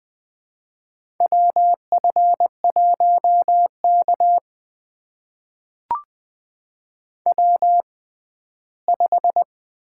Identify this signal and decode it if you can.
Morse code, heard as a keyed tone, about 20 wpm, spaced as for 14 wpm, 700 Hz. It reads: WF1K W5
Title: USA Call Signs - 4 Characters - 6x Character Spacing - 20wpm